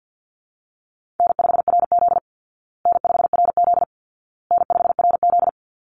Morse code, heard as a keyed tone, about 49 wpm, 700 Hz. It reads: N5RZ N5RZ N5RZ
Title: Easy Call Signs Repeated 3x - 50wpm